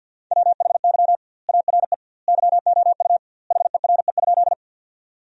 Morse code, heard as a keyed tone, about 48 wpm, 700 Hz. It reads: WHY ARE YOU HERE?
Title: 2x Word-Spacing - Sentences from Top 200 Words - 50wpm